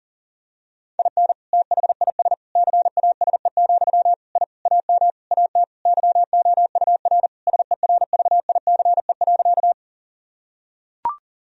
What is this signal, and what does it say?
Morse code, heard as a keyed tone, about 40 wpm, 700 Hz. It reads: IN THIS CASE, I AM AT YOUR SERVICE.
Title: Sentences from Top 1000 Words - Review Part 4 - 40wpm